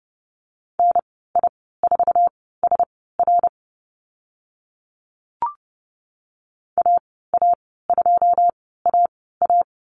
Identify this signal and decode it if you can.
Morse code, heard as a keyed tone, about 30 wpm, spaced as for 17 wpm, 700 Hz. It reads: NI4SR AA2AA
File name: Lesson 14g - Number 2 - Call Signs 30wpm